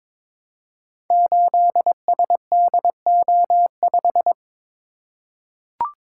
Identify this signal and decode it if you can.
Morse code, heard as a keyed tone, about 22 wpm, 700 Hz. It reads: 8SDO5
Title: Code Groups - 5 Letter:Numbers - 22wpm